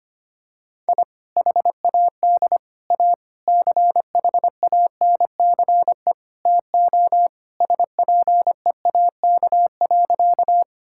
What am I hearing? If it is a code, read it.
Morse code, heard as a keyed tone, about 25 wpm, 700 Hz. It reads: I HAD A CHANCE TO SPEAK.
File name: practice / 25wpm / Sentences with 4-Word Phrases - 25wpm